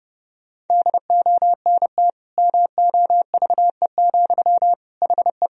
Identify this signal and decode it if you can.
Morse code, heard as a keyed tone, about 30 wpm, 700 Hz. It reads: DONT MOVE, HE
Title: Sentences from Top 1000 Words - Review Part 1 - 30wpm